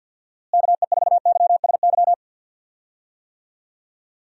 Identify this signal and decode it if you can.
Morse code, heard as a keyed tone, about 46 wpm, 700 Hz. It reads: KE4YSY
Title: USA Call Signs - 6 Characters - Repeated 2x - 50wpm